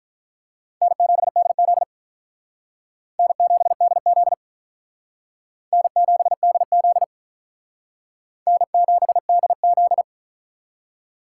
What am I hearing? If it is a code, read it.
Morse code, heard as a keyed tone, about 45 wpm, 700 Hz. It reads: N7DZ N7DZ N7DZ N7DZ